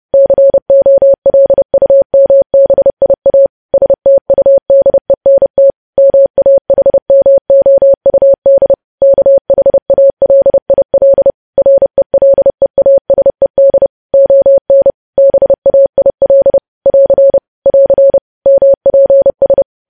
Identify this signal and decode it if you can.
Morse code, heard as a keyed tone, about 30 wpm, 550 Hz. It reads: COLUMBIA STUDENT MAHMOUD KHALIL RELEASED ON BAIL <AR> <AR> MPS
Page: news headlines in morse code at 30 wpm